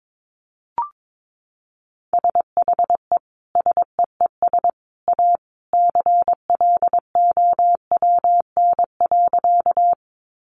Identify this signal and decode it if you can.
Morse code, heard as a keyed tone, about 22 wpm, 700 Hz. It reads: SHE SEES A CLOWN.
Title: Easy Phrases 22wpm